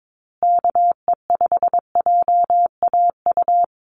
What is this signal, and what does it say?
Morse code, heard as a keyed tone, about 22 wpm, 700 Hz. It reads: KE5JAU
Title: USA Call Signs Repeated 2x 22wpm